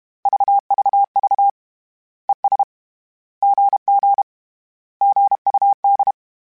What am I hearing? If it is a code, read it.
Morse code, heard as a keyed tone, about 31 wpm, 800 Hz. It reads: VVV ES GG GUD
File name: Short Runs / Short Runs-mp3s / Words @30